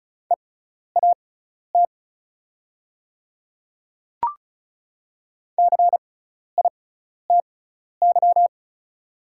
Morse code, heard as a keyed tone, about 34 wpm, spaced as for 12 wpm, 690 Hz. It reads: EAT CITY
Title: 6x Character Spacing - Top 300 Words - 35wpm